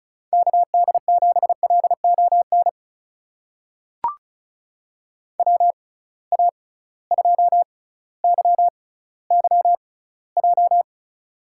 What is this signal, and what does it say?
Morse code, heard as a keyed tone, about 36 wpm, spaced as for 18 wpm, 700 Hz. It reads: KD7LON WA2YYJ
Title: USA Call Signs - 6 Characters - 6x Character Spacing - 35wpm